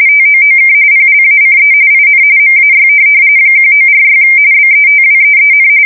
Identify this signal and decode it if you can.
RTTY, about 45.45 baud. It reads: TOOK PLACE IN MAY 1946 BETWEEN
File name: fsk170 / teletype-RTTY